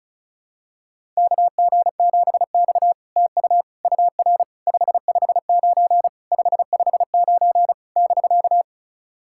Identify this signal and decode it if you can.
Morse code, heard as a keyed tone, about 35 wpm, 700 Hz. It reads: KG7X TU UR 559 559 <BK>
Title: SOTA QSO Practice 35wpm